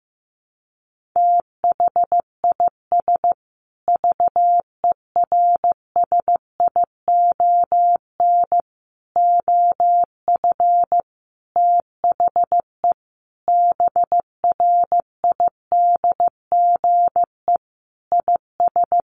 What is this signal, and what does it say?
Morse code, heard as a keyed tone, about 15 wpm, 700 Hz. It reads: THIS VERSION OF THE BRIDGE IS
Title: Sentences from Top 1400 Words 15wpm